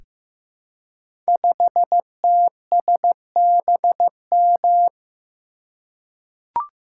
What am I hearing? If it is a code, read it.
Morse code, heard as a keyed tone, about 15 wpm, 700 Hz. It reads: 5TSBM
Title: Code Groups - 5 Letter:Numbers - 15wpm